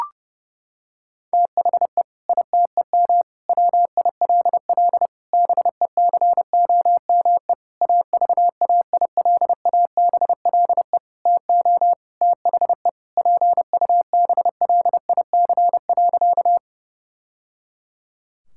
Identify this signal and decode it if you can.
Morse code, heard as a keyed tone, about 30 wpm, 700 Hz. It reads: THE ITEM WILL BECOME AVAILABLE TO THE PUBLIC.
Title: Sentences from Top 1000 Words - Review Part 6 - 30wpm